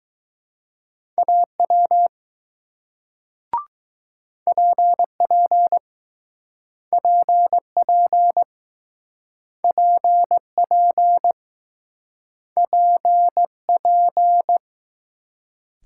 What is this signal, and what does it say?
Morse code, heard as a keyed tone, about 20 wpm, 700 Hz. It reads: AW PP PP PP PP